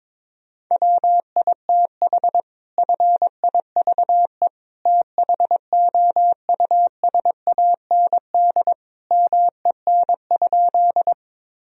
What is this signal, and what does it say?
Morse code, heard as a keyed tone, about 22 wpm, 700 Hz. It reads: WITH FIVE THOUSAND MEN?
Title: Sentences from Top 1000 Words - Review Part 3 - 22wpm